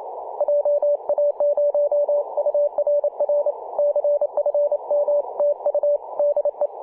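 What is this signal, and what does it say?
Morse code, heard as a keyed tone, about 28 wpm, 580 Hz. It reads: JA0URR CFM TU DE
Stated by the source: P4/DL6RAI heard on CW on 7.003 MHz, recorded by ja0rug.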